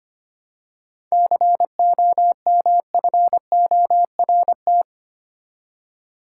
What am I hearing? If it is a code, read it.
Morse code, heard as a keyed tone, about 25 wpm, 700 Hz. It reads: COMFORT
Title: Seven Letter Words - 25wpm